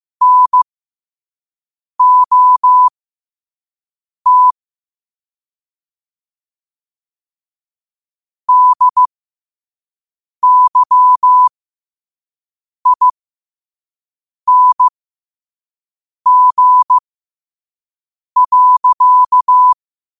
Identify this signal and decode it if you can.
Morse code, heard as a keyed tone, about 15 wpm, spaced as for 5 wpm, 1000 Hz. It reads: NOT DYING.
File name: qotdfiles / 5wpm